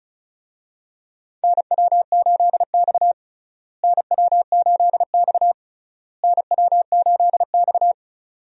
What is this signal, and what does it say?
Morse code, heard as a keyed tone, about 34 wpm, 700 Hz. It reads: NW8X NW8X NW8X
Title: USA Call Signs - 4 Characters - Repeated 3x - 35wpm